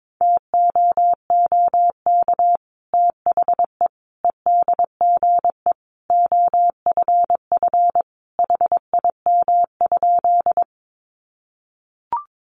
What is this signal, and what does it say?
Morse code, heard as a keyed tone, about 22 wpm, 700 Hz. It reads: TOOK THE EDGE OFF HIM?
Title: Sentences from Top 1300 Words 22wpm